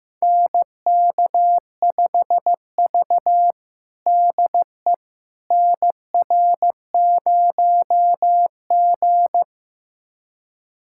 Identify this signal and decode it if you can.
Morse code, heard as a keyed tone, about 15 wpm, 700 Hz. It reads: NK5V DE NR0G
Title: USA Callsign DE Callsign - 4 Characters - 15wpm